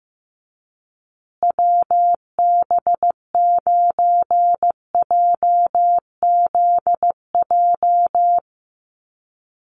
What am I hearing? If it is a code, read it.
Morse code, heard as a keyed tone, about 15 wpm, 700 Hz. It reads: WB9JZJ
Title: USA Call Signs Repeated 3x 15wpm